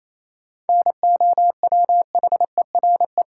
Morse code, heard as a keyed tone, about 28 wpm, 700 Hz. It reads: NOWHERE
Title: Sets of 5 Words Encore - New Word Drill - 28wpm